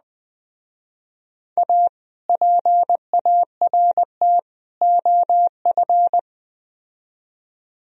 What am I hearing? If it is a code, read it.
Morse code, heard as a keyed tone, about 21 wpm, 700 Hz. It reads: A PART OF